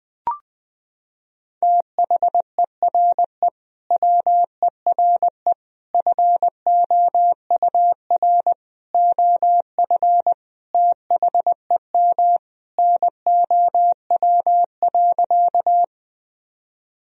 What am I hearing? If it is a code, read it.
Morse code, heard as a keyed tone, about 20 wpm, 700 Hz. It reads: THERE WERE FOUR OF THEM NOW.
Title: Sentences from Top 1000 Words - Review Part 1 - 20wpm